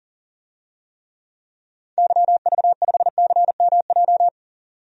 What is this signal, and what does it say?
Morse code, heard as a keyed tone, about 40 wpm, 700 Hz. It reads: YV5CMJ
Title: DXCC-100 Call Signs - 6 Characters - 40wpm